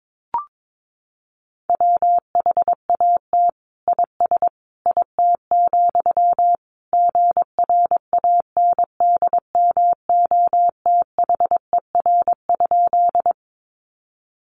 Morse code, heard as a keyed tone, about 22 wpm, 700 Hz. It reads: WHAT IS IT, GRANDMOTHER?
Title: Sentences from Top 1200 Words 22wpm